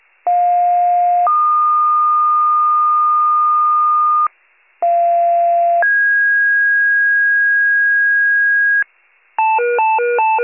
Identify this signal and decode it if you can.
Two-tone page, heard as a one-second tone, then a three-second tone, about 710 then 1220 Hz, about 690 then 1650 Hz, with noise. It is heard from 0.3 to 8.8 s.